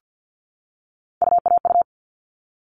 Morse code, heard as a keyed tone, about 50 wpm, 700 Hz. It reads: UAU
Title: All Cut Numbers - 3-Digit 50wpm